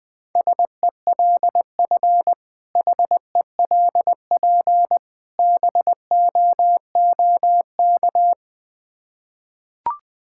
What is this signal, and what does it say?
Morse code, heard as a keyed tone, about 20 wpm, 700 Hz. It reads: SELF HELP BOOK
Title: Difficult Words with Hints - 20wpm